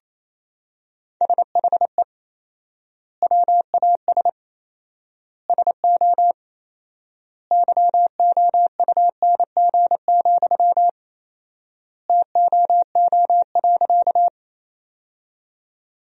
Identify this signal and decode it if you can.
Morse code, heard as a keyed tone, about 28 wpm, 700 Hz. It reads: SHE WAS SO YOUNG, TOO.
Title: 4x Word-Spacing - Sentences from Top 400 Words - 28wpm